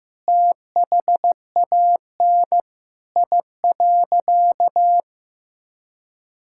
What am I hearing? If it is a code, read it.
Morse code, heard as a keyed tone, about 15 wpm, 700 Hz. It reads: THAN I.
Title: Sentences from Top 1000 Words - Review Part 5 - 15wpm